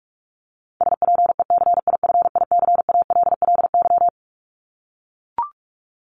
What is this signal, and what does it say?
Morse code, heard as a keyed tone, about 44 wpm, 700 Hz. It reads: SPECIFICALLY
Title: Top 1001-1250 Words 45wpm